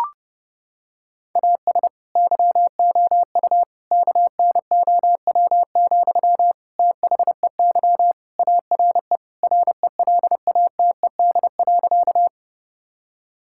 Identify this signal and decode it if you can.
Morse code, heard as a keyed tone, about 30 wpm, 700 Hz. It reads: AS YOU KNOW, THEY ARE RELATED.
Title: Sentences from Top 1000 Words - Review Part 5 - 30wpm